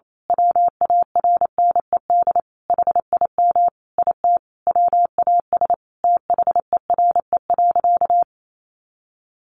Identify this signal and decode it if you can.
Morse code, heard as a keyed tone, about 28 wpm, 700 Hz. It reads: WARNED HIM IT WAS THERE.
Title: A Sound Of Thunder 28wpm